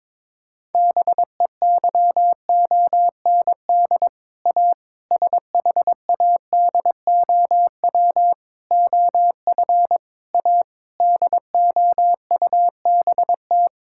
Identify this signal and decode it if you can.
Morse code, heard as a keyed tone, about 22 wpm, 700 Hz. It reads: BEYOND A SHADOW OF A DOUBT
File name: practice / 22wpm / Sets of 7 Words - 22wpm